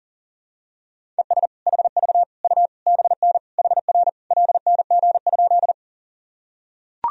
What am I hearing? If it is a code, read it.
Morse code, heard as a keyed tone, about 40 wpm, 700 Hz. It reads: ES HV U BN HR LNG?
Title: Sentences from Top 200 Words and Abbreviations 40wpm